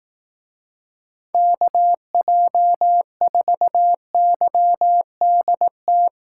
Morse code, heard as a keyed tone, about 18 wpm, 700 Hz. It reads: KJ4YDT
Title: Speed Racing - USA 2x3 Call Signs 15wpm